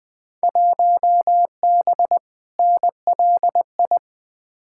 Morse code, heard as a keyed tone, about 20 wpm, 700 Hz. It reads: 1B NLI
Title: Field Day Exchanges Encore - 20wpm